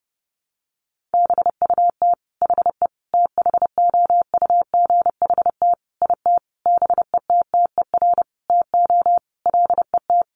Morse code, heard as a keyed tone, about 30 wpm, 700 Hz. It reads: BUT HE THOUGHT IT BETTER TO LET
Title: Sentences from Top 1400 Words 30wpm